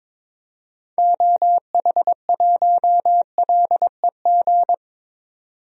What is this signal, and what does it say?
Morse code, heard as a keyed tone, about 22 wpm, 700 Hz. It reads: OH1LEG
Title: DXCC-100 Call Signs - 6 Characters - 22wpm